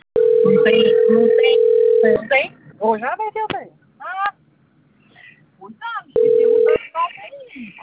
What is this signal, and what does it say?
Ringback tone (North America), 440 + 480 Hz, with speech.
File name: out- - - - - .73562